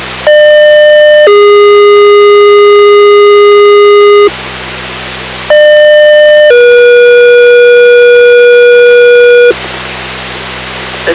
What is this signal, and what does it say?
Two-tone page, heard as a one-second tone, then a three-second tone, about 600 then 410 Hz, about 600 then 480 Hz, with speech. It is heard from 0.3 to 9.5 s.